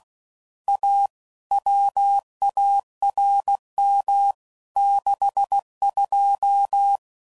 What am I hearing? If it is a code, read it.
Morse code, heard as a keyed tone, about 16 wpm, 800 Hz. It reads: A WARM 62
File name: CW Code Groups @15WPM/Weather Sets